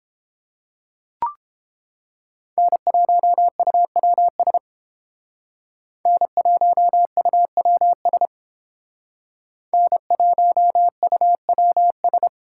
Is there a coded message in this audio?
Yes: Morse code from 2.6 to 12.3 s, heard as a keyed tone, about 30 wpm, 700 Hz. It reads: N1UWS N1UWS N1UWS